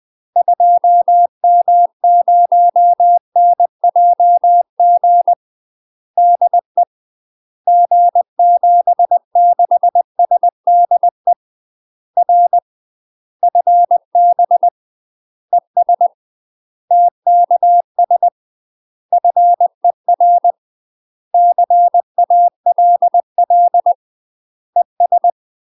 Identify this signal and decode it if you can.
Morse code, heard as a keyed tone, about 20 wpm, 700 Hz. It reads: 2M0NJG DE G76SDE R FB ES TKS FER CALL ES
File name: UK MASTER QSOs TRAINING/20WPM